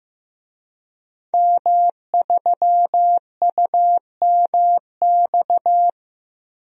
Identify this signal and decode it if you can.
Morse code, heard as a keyed tone, about 15 wpm, 700 Hz. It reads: M3UMX
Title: DXCC-100 Call Signs - 5 Characters - Repeated 3x - 15wpm